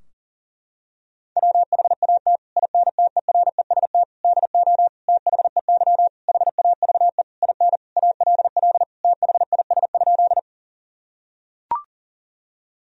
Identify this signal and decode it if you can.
Morse code, heard as a keyed tone, about 40 wpm, 700 Hz. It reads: WHAT INTEREST DO THEY HAVE IN ALL THIS?
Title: Sentences from Top 1000 Words - Review Part 2 - 40wpm